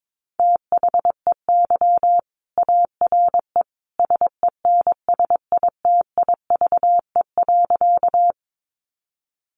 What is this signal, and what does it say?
Morse code, heard as a keyed tone, about 22 wpm, 700 Hz. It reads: THEY ARE SENSITIVE.
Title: Sentences from Top 1400 Words 22wpm